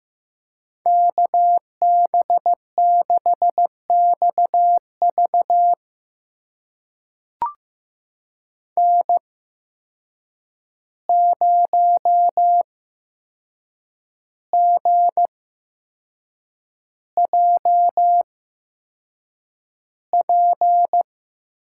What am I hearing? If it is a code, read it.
Morse code, heard as a keyed tone, about 15 wpm, spaced as for 6 wpm, 700 Hz. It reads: KB6XV N0GJP
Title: USA Call Signs - 5 Characters - 8x Character Spacing - 15wpm